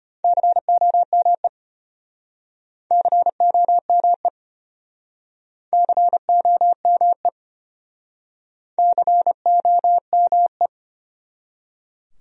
Morse code, heard as a keyed tone, about 35 wpm, 700 Hz. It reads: COME COME COME COME